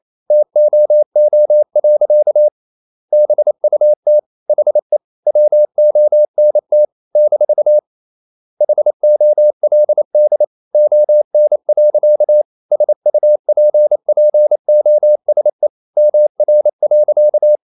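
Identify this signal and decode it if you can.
Morse code, heard as a keyed tone, about 28 wpm, 600 Hz. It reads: TOO. BUT HE WONT - HOLD ON. SUPPOSE MR.
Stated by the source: Morse code practice at 28 wpm, text of Ken Follett's novel Never in English, from CW med Gnister.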